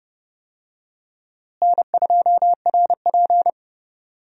Morse code, heard as a keyed tone, about 30 wpm, 700 Hz. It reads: N2RP
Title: USA Call Signs - 4 Characters - Repeated 3x - 30wpm